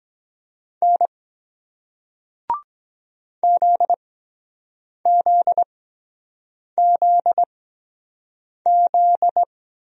Morse code, heard as a keyed tone, about 23 wpm, spaced as for 7 wpm, 700 Hz. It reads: N ZZZZ